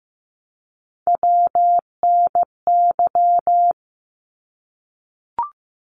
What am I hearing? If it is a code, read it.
Morse code, heard as a keyed tone, about 15 wpm, 700 Hz. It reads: WNY
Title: ARRL Sections 15wpm